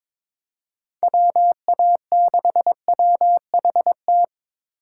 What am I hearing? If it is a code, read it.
Morse code, heard as a keyed tone, about 22 wpm, 700 Hz. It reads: WA6WHT
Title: USA Call Signs - 6 Characters - 3x Character Spacing - 22wpm